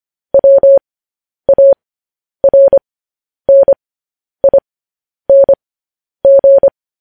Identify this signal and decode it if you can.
Morse code, heard as a keyed tone, about 25 wpm, spaced as for 10 wpm, 550 Hz. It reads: WARNING
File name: podcast / News headlines-1x-25x10